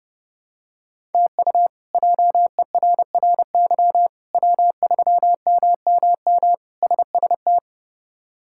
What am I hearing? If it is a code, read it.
Morse code, heard as a keyed tone, about 30 wpm, 700 Hz. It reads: TU JERRY W3MMM SST
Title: K1USN SST QSO Practice - 30wpm